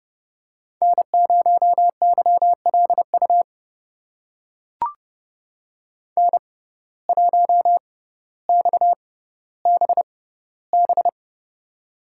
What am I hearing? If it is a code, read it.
Morse code, heard as a keyed tone, about 30 wpm, spaced as for 15 wpm, 700 Hz. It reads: N0YLU N1XBB
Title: USA Call Signs - 5 Characters - 6x Character Spacing - 30wpm